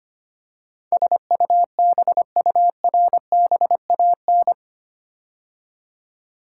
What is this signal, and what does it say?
Morse code, heard as a keyed tone, about 25 wpm, 700 Hz. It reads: SUBURBAN
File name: 8 Letter Words - First 3 Letters Given - 25wpm